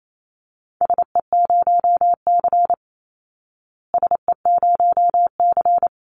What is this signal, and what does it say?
Morse code, heard as a keyed tone, about 28 wpm, 700 Hz. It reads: SE0C SE0C
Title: Call Signs Repeated 3x - 28wpm